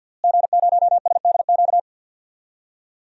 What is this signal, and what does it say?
Morse code, heard as a keyed tone, about 46 wpm, 700 Hz. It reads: G0SDQ
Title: DXCC-100 Call Signs - 5 Characters - Repeated 3x - 50wpm